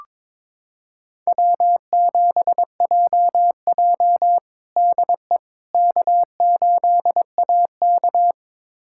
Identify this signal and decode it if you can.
Morse code, heard as a keyed tone, about 22 wpm, 700 Hz. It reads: W7JJ DE K8AK
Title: USA Callsign DE Callsign - 4 Characters - 22wpm